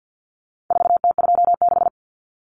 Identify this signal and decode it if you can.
Morse code, heard as a keyed tone, about 48 wpm, 700 Hz. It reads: 4T26
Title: Common Cut Numbers - 4-Digit 50wpm